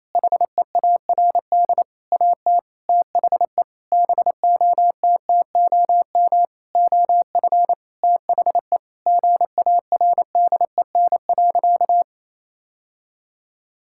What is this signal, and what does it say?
Morse code, heard as a keyed tone, about 27 wpm, 700 Hz. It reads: HEARD AT THE BOTTOM OF THE GARDEN.
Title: Sentences from Top 1000 Words - Review Part 6 - 28wpm